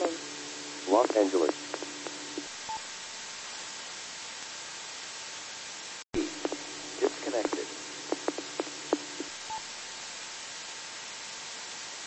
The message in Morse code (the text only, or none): none